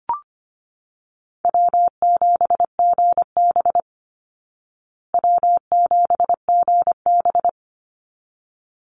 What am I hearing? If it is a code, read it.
Morse code, heard as a keyed tone, about 25 wpm, 700 Hz. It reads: W7GB W7GB
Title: Easy Call Signs Repeated 2x - 25wpm